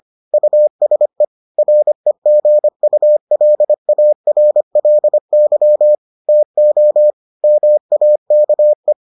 Morse code, heard as a keyed tone, about 25 wpm, 600 Hz. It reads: USE REGULARLY TO MAKE